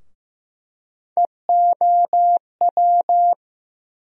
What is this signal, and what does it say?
Morse code, heard as a keyed tone, about 15 wpm, 700 Hz. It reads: EOW
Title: USA Call Sign Postfixes - 3 Letters 15wpm